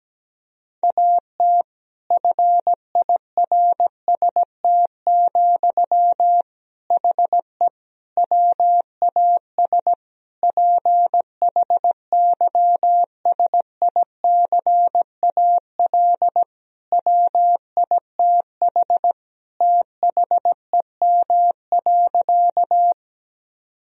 Morse code, heard as a keyed tone, about 17 wpm, 700 Hz. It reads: AT FIRST, HE WAS PHYSICAL WITH THEM.
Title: Sentences from Top 1000 Words - Review Part 4 - 17wpm